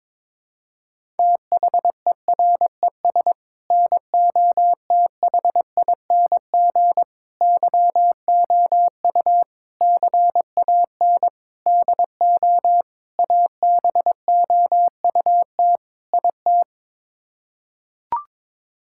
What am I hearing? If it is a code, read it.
Morse code, heard as a keyed tone, about 22 wpm, 700 Hz. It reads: THERES NOTHING YOU CAN DO ABOUT IT
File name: practice / 22wpm / Sets of 7 Words - 22wpm